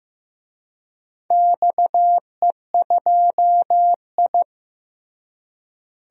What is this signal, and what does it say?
Morse code, heard as a keyed tone, about 16 wpm, 700 Hz. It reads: XE2I
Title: DXCC-100 Call Signs - 4 Characters - Repeated 3x - 15wpm